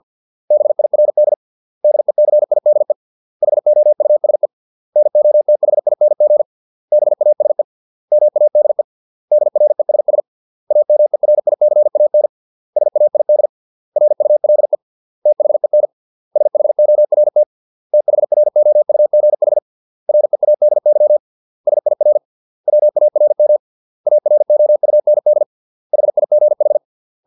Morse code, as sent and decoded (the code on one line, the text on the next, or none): -... .. .-. -.. / -.. -.-. .. -.. / .... --- ..- ... / -. --- - .... .. -. --. / -... .- ... / -- .- -.. / -.. .-. ... ... / .- -- . .-. .. -.-. .- -. / ... .- .. -.. / .-. ..- .-.. / - .... -. / ... .... --- .-. - / - .... .-. --- ..- --. .... / .-. .- -.. -.-- / ... .. .-. / .-- .- .-. -- / .- .-. --- ..- -. -.. / .... .. --. ....
BIRD DECIDE HOUSE NOTHING BASE MADE DRESS AMERICAN SAID RULE THEN SHORT THROUGH READY SIR WARM AROUND HIGH